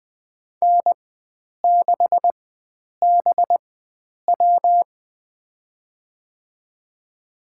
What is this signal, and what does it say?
Morse code, heard as a keyed tone, about 20 wpm, spaced as for 9 wpm, 700 Hz. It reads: N6BW
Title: USA Call Signs - 4 Characters - 4x Character Spacing - 20wpm